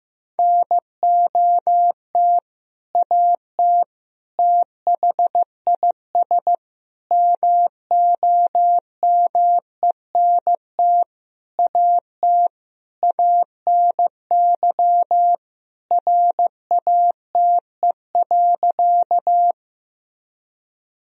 Morse code, heard as a keyed tone, about 15 wpm, 700 Hz. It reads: NOT AT THIS MOMENT AT ANY RATE.
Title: Sentences from Top 1000 Words - Review Part 4 - 15wpm